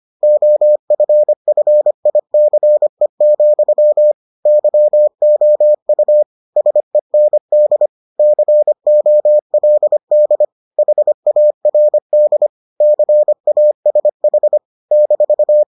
Morse code, heard as a keyed tone, about 25 wpm, 600 Hz. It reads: OFFICE, YOU SEND COLD HARD CASH -